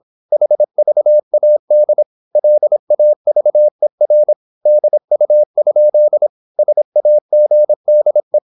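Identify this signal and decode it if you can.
Morse code, heard as a keyed tone, about 26 wpm, 600 Hz. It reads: HVAD LAVER DU? SAGDE
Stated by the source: High-speed (QRQ) Morse at 26 wpm.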